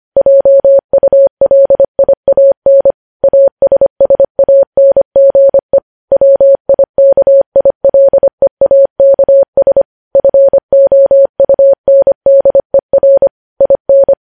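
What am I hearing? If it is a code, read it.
Morse code, heard as a keyed tone, about 25 wpm, 550 Hz. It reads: JULIAN ASSANGE WIKILEAKS FOUNDER IN
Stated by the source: Morse code news headlines sent at 25 wpm.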